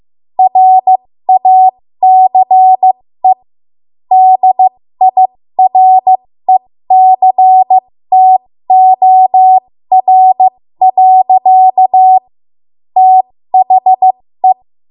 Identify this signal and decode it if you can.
Morse code, heard as a keyed tone, about 15 wpm, spaced as for 13 wpm, 750 Hz. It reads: RACE DIRECTOR. THE
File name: Morse / 13WPM